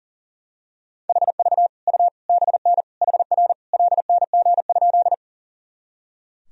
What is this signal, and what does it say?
Morse code, heard as a keyed tone, about 40 wpm, 700 Hz. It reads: HV U BN HR LNG?
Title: Sentences from Top 200 Words and Abbreviations 40wpm